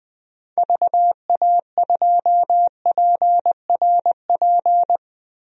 Morse code, heard as a keyed tone, about 20 wpm, 700 Hz. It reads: VA2PRP
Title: DXCC-100 Call Signs - 6 Characters - Repeated 2x - 20wpm